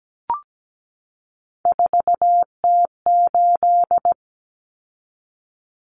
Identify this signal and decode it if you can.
Morse code, heard as a keyed tone, about 17 wpm, 690 Hz. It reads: 4T8
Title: Common Cut Numbers - 3-Digit 17wpm